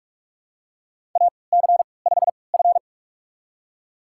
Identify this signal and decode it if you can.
Morse code, heard as a keyed tone, about 46 wpm, spaced as for 26 wpm, 700 Hz. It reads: AC5F
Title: USA Call Signs - 4 Characters - 3x Character Spacing - 45wpm